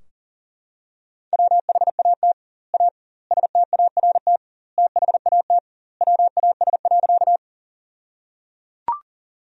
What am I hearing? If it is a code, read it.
Morse code, heard as a keyed tone, about 37 wpm, 700 Hz. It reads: WHAT A START THAT WAS.
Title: 2x Word-Spacing - Sentences from Top 300 Words - 40wpm